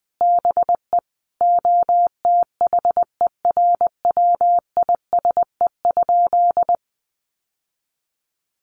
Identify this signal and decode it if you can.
Morse code, heard as a keyed tone, about 20 wpm, 700 Hz. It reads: BE OTHERWISE?
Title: Sentences from Top 1200 Words 20wpm